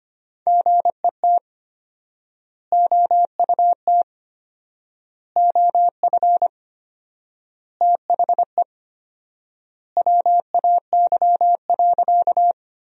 Morse code, heard as a keyed tone, about 25 wpm, 700 Hz. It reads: GET OUT OF THE WAY.
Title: 4x Word-Spacing - Sentences from Top 200 Words - 25wpm